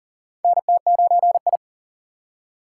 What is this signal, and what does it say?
Morse code, heard as a keyed tone, about 42 wpm, 700 Hz. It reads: NT9I
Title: USA Call Signs - 4 Characters - 6x Character Spacing - 40wpm